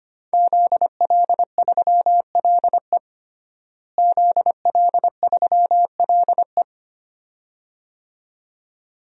Morse code, heard as a keyed tone, about 25 wpm, 700 Hz. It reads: ZL3LE ZL3LE
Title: DXCC-100 Call Signs - 5 Characters - Repeated 2x - 25wpm